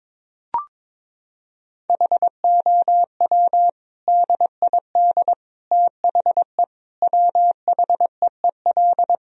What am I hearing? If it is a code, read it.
Morse code, heard as a keyed tone, about 22 wpm, 700 Hz. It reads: HOW DID THE WHEEL
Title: Sentences from Top 1000 Words - Review Part 3 - 22wpm